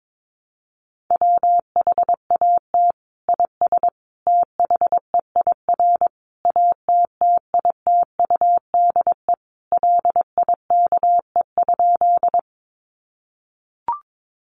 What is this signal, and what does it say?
Morse code, heard as a keyed tone, about 22 wpm, 700 Hz. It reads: WHAT IS THEIR ATTITUDE LIKE?
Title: Sentences from Top 1300 Words 22wpm